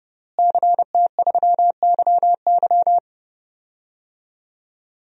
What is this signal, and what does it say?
Morse code, heard as a keyed tone, about 32 wpm, 700 Hz. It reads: CT3YY
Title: DXCC-100 Call Signs - 5 Characters - Repeated 3x - 30wpm